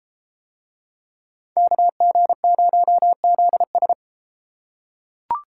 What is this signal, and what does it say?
Morse code, heard as a keyed tone, about 33 wpm, 700 Hz. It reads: KG0ZS